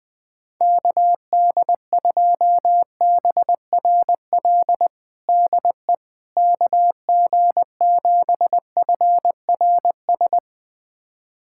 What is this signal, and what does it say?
Morse code, heard as a keyed tone, about 21 wpm, 700 Hz. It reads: KD2BRL DE KG7FRS
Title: USA Callsign DE Callsign 20wpm